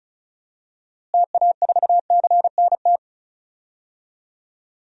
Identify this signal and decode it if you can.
Morse code, heard as a keyed tone, about 34 wpm, 700 Hz. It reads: TA4CNT
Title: DXCC-100 Call Signs - 6 Characters - Repeated 2x - 35wpm